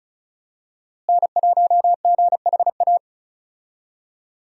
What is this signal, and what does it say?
Morse code, heard as a keyed tone, about 35 wpm, 700 Hz. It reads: N1GHA
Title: USA Call Signs - 5 Characters - Repeated 2x - 35wpm